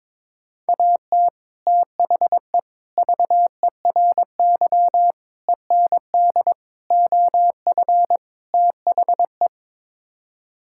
Morse code, heard as a keyed tone, about 22 wpm, 700 Hz. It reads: AT THE VERY END OF THE
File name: practice / 22wpm / Sets of 6 Words - 22wpm